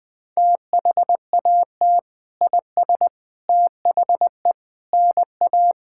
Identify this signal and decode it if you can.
Morse code, heard as a keyed tone, about 21 wpm, 700 Hz. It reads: THAT IS THE NA
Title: Sentences from Top 1000 Words - Review Part 5 - 20wpm